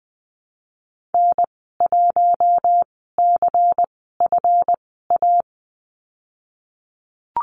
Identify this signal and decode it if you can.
Morse code, heard as a keyed tone, about 19 wpm, spaced as for 14 wpm, 700 Hz. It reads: N1CFA
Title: 173 Lesson 28g - Number 9 - Call Signs 20wpm